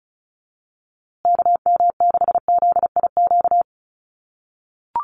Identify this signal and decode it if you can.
Morse code, heard as a keyed tone, about 34 wpm, 700 Hz. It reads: KM6ZIQ
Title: USA Call Signs Repeated 2x 35wpm